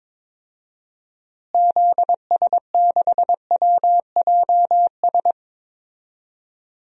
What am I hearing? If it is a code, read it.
Morse code, heard as a keyed tone, about 22 wpm, 700 Hz. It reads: ZS6WJS
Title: DXCC-100 Call Signs - 6 Characters - Repeated 3x - 22wpm